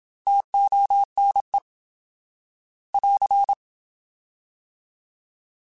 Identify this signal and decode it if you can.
Morse code, heard as a keyed tone, about 27 wpm, 800 Hz. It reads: TONE <AR>
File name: Alphabetical / Alphabetical mp3s / Ts @25